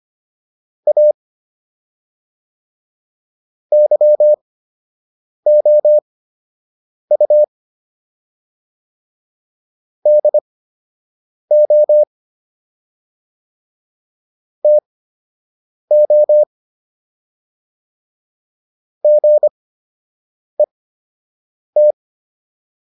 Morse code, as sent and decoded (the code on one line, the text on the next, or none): .- / -.-- --- ..- / -.. --- / - --- / --. -
A YOU DO TO GET